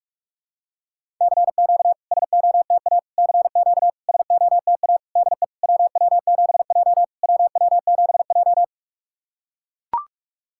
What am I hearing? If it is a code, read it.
Morse code, heard as a keyed tone, about 44 wpm, 690 Hz. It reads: CQ SOTA CQ SOTA DE WW7J WW7J
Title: SOTA QSO Practice 45wpm